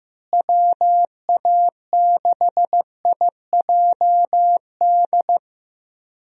Morse code, heard as a keyed tone, about 16 wpm, 700 Hz. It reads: WA6IJD
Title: USA Call Signs - 6 Characters - Repeated 3x - 15wpm